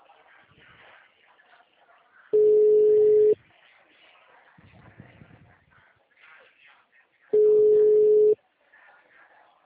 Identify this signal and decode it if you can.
Ringback tone (Europe, 425 Hz).